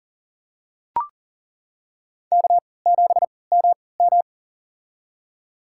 Morse code, heard as a keyed tone, about 39 wpm, spaced as for 22 wpm, 700 Hz. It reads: K7MM
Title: USA Call Signs - 4 Characters - 3x Character Spacing - 40wpm